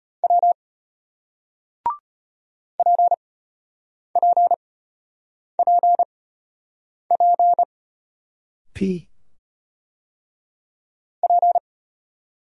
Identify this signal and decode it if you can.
Morse code, heard as a keyed tone, about 33 wpm, spaced as for 8 wpm, 700 Hz. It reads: W PPPP P